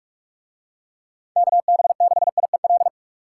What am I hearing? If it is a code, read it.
Morse code, heard as a keyed tone, about 46 wpm, 700 Hz. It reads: KB6IEL